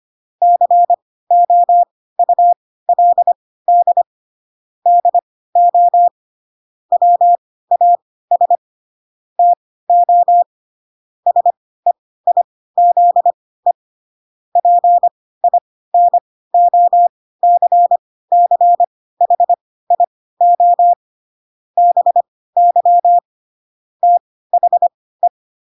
Morse code, heard as a keyed tone, about 25 wpm, spaced as for 16 wpm, 700 Hz. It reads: COULD DO WAS TO SEIZE PINOCCHIO BY THE
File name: Pinocchio / Pinocchio ch3 10 25 ik1hge